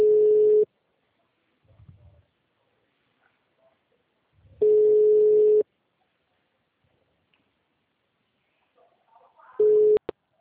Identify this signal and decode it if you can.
Ringback tone (Europe, 425 Hz).